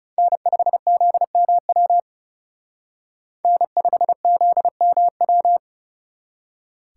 Morse code, heard as a keyed tone, about 38 wpm, 700 Hz. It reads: N5ZMW N5ZMW